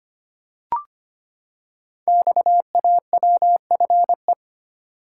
Morse code, heard as a keyed tone, about 25 wpm, 700 Hz. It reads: XAWFE
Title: Code Groups - 5 Letters - 25wpm